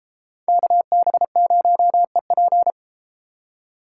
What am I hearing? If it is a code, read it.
Morse code, heard as a keyed tone, about 33 wpm, 700 Hz. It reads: KB0EP